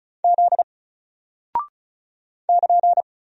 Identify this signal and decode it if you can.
Morse code, heard as a keyed tone, about 36 wpm, 700 Hz. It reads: Z <KN>
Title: Single Character (Letter:Number:Punctuation:Prosign) Alt 35wpm